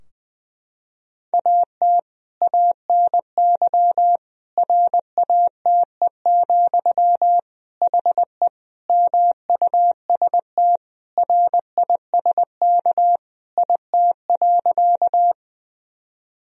Morse code, heard as a keyed tone, about 20 wpm, 700 Hz. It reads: AT ANY RATE, HE MUST RISK IT.
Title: Sentences from Top 1000 Words - Review Part 4 - 20wpm